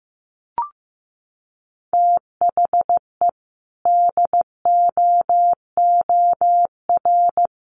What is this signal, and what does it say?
Morse code, heard as a keyed tone, about 15 wpm, 700 Hz. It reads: THE DOOR
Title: Sets of 2 Words 15wpm